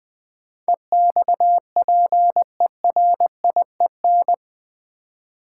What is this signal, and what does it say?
Morse code, heard as a keyed tone, about 20 wpm, 700 Hz. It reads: EXPERIEN
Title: Build A Word - 4 Letters At A Time - 10-Letter Words - 20wpm